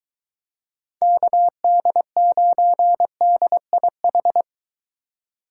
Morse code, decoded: KD9DIH

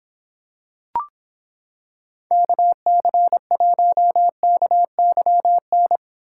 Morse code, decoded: KC1KYN

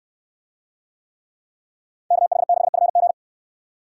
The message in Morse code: KH6FG

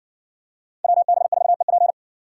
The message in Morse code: WB4EP

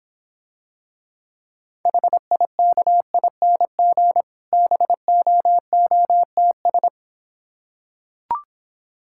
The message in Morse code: HIKING BOOTS

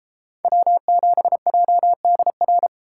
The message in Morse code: W7JDR

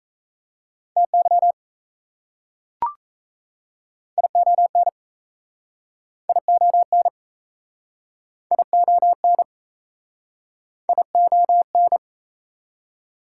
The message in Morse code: TY ION ION ION ION

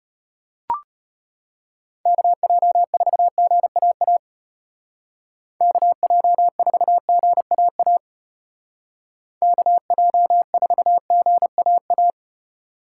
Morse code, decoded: KJ4GAA KJ4GAA KJ4GAA

